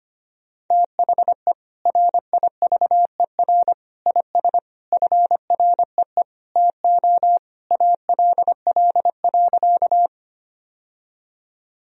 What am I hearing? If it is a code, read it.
Morse code, heard as a keyed tone, about 25 wpm, 700 Hz. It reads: THE RIVER IS FREE TO ALL.
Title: Sentences from Top 1000 Words - Review Part 1 - 25wpm